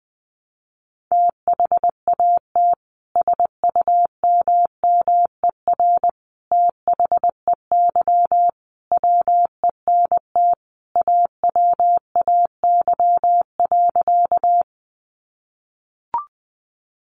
Morse code, heard as a keyed tone, about 20 wpm, 700 Hz. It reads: THAT SUMMER THEY WENT AWAY.
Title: Sentences from Top 1100 Words 20wpm